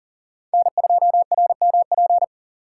Morse code, heard as a keyed tone, about 40 wpm, 700 Hz. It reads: N2RMP
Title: USA Call Signs - 5 Characters - 4x Character Spacing - 40wpm